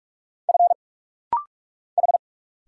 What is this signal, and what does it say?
Morse code, heard as a keyed tone, about 46 wpm, 700 Hz. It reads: F H
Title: Single Letter-Number Alt - Rapid-Fire 45wpm